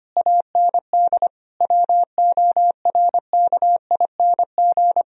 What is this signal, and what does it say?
Morse code, heard as a keyed tone, about 25 wpm, 700 Hz. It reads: AND WORKING
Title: Sentences from Top 1000 Words - Review Part 4 - 25wpm